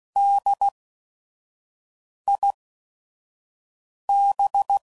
Morse code, heard as a keyed tone, about 16 wpm, spaced as for 5 wpm, 800 Hz. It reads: DIB
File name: Prefixes / By First Letter / Ds